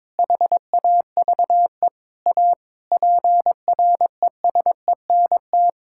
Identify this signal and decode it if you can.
Morse code, heard as a keyed tone, about 22 wpm, 700 Hz. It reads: HAVE A PRESENT
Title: Sentences from Top 1000 Words - Review Part 5 - 22wpm